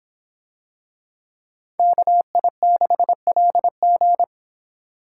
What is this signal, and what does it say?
Morse code, heard as a keyed tone, about 26 wpm, 700 Hz. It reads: KI6LG